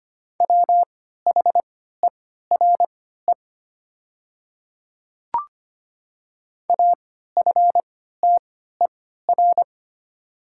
Morse code, heard as a keyed tone, about 25 wpm, spaced as for 14 wpm, 700 Hz. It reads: WHERE AFTER
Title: 3x Character Spacing - Top 200 Words - 25wpm